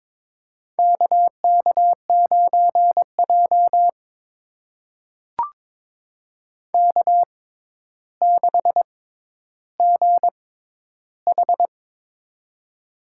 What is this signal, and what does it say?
Morse code, heard as a keyed tone, about 21 wpm, spaced as for 11 wpm, 700 Hz. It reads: KK9J K6GH
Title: USA Call Signs - 4 Characters - 6x Character Spacing - 22wpm